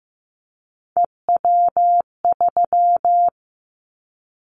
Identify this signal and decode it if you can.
Morse code, heard as a keyed tone, about 15 wpm, 700 Hz. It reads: EW3
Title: Callsign Prefixes for Top 100 DXCC - 15wpm